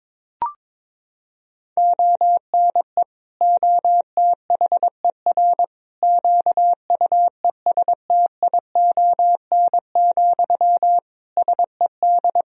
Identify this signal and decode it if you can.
Morse code, heard as a keyed tone, about 21 wpm, 700 Hz. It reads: ONE OTHER QUESTION, SED
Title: Sentences from Top 400 Words and Abbreviations 22wpm